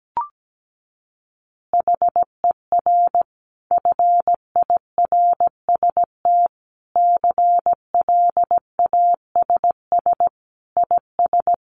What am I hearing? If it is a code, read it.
Morse code, heard as a keyed tone, about 17 wpm, 700 Hz. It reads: HER FIRST CLASS IS
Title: Additional Easy Phrases 17wpm